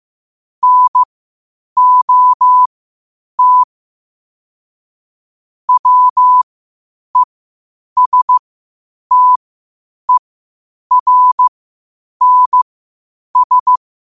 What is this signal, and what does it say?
Morse code, heard as a keyed tone, about 15 wpm, spaced as for 8 wpm, 1000 Hz. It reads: NOT WESTERNS
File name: qotdfiles / 8wpm